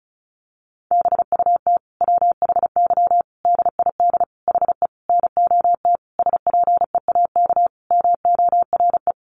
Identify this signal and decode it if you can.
Morse code, heard as a keyed tone, about 34 wpm, 700 Hz. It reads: BUT WHY DID HE NOT SPEAK MORE
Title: Sentences from Top 1300 Words 35wpm